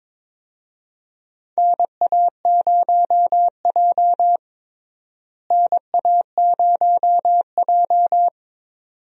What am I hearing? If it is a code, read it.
Morse code, heard as a keyed tone, about 21 wpm, 700 Hz. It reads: NA0J NA0J
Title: USA Call Signs - 4 Characters - Repeated 2x - 22wpm